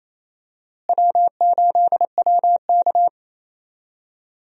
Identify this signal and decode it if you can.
Morse code, heard as a keyed tone, about 27 wpm, 700 Hz. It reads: W8WK
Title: USA Call Signs - 4 Characters - Repeated 2x - 28wpm